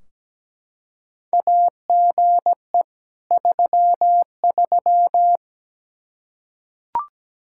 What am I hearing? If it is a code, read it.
Morse code, heard as a keyed tone, about 17 wpm, 700 Hz. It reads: AGE 33
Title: K1USN SST QSO Practice 17wpm